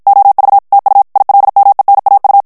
Morse code, heard as a keyed tone, about 54 wpm, 800 Hz. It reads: OV TU ILMENAU
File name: morse mp3 / ALLEMAND texte clair 2 vitesse 50 mots minute